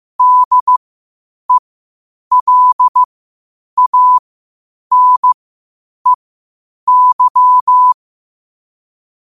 Morse code, heard as a keyed tone, about 15 wpm, spaced as for 8 wpm, 1000 Hz. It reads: DELANEY
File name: qotdfiles / 8wpm